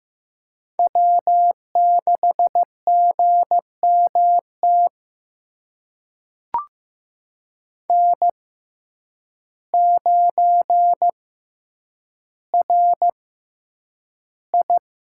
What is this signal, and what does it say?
Morse code, heard as a keyed tone, about 15 wpm, spaced as for 8 wpm, 700 Hz. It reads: W6GMT N9RI
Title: USA Call Signs - 6x Character Spacing - 15wpm